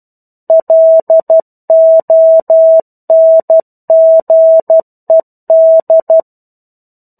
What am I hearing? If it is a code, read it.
Morse code, heard as a keyed tone, about 12 wpm, 650 Hz. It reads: LONGED
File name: 01-Chapter-01 12 WPM